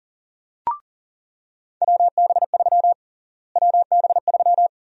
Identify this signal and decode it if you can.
Morse code, heard as a keyed tone, about 40 wpm, 700 Hz. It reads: WB3 WB3
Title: USA Call Sign Prefixes - Repeated 2x - 40wpm